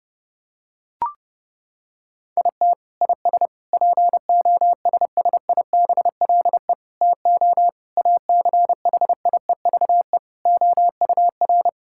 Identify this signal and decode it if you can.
Morse code, heard as a keyed tone, about 30 wpm, 700 Hz. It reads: IT IS POSSIBLE TO ACHIEVE OUR
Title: Sentences from Top 1000 Words - Review Part 6 - 30wpm